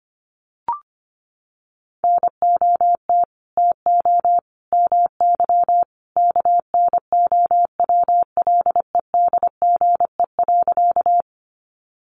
Morse code, heard as a keyed tone, about 25 wpm, 700 Hz. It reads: NOT TO MY KNOWLEDGE.
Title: Sentences from Top 600 Words 25wpm